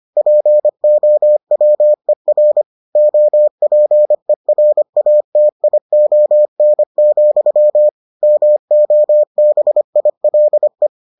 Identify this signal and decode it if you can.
Morse code, heard as a keyed tone, about 25 wpm, 600 Hz. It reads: POWER OPERATION, MOBILE